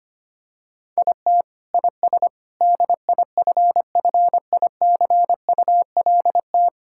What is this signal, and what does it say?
Morse code, heard as a keyed tone, about 25 wpm, 700 Hz. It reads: IT IS DIFFICULT
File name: Sets of 9 Words - 25wpm